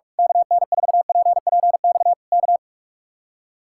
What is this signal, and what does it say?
Morse code, heard as a keyed tone, about 46 wpm, 700 Hz. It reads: KN4PPX K
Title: Speed-Racing Common QSO Phrases - 45wpm